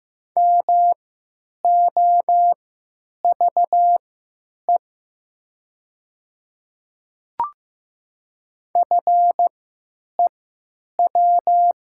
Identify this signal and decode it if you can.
Morse code, heard as a keyed tone, about 15 wpm, spaced as for 9 wpm, 700 Hz. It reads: MOVE FEW
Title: 3x Character Spacing - Top 300 Words - 15wpm